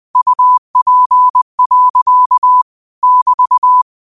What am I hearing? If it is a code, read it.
Morse code, heard as a keyed tone, about 20 wpm, 1000 Hz. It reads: UP. <BT>
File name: qotdfiles / 20wpm